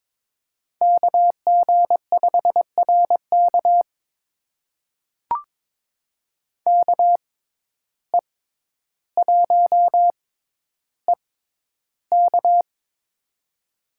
Morse code, heard as a keyed tone, about 22 wpm, spaced as for 11 wpm, 700 Hz. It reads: KG5RK KE1EK
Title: USA Call Signs - 5 Characters - 6x Character Spacing - 22wpm